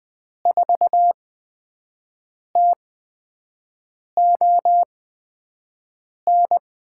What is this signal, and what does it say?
Morse code, heard as a keyed tone, about 20 wpm, spaced as for 5 wpm, 700 Hz. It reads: 4TON
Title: USA Call Signs - 6 Characters - 8x Character Spacing - 20wpm